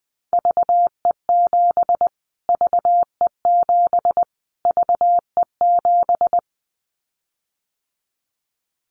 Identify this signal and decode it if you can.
Morse code, heard as a keyed tone, about 20 wpm, 700 Hz. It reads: VE7 VE7 VE7
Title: Callsign Prefixes for Top 25 DXCC - Repeated 3x - 20wpm